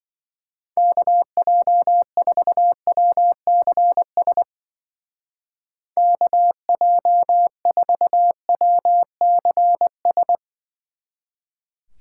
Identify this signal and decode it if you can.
Morse code, heard as a keyed tone, about 23 wpm, 700 Hz. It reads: KJ4WCS KJ4WCS
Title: Speed Racing - USA 2x3 Call Signs 20wpm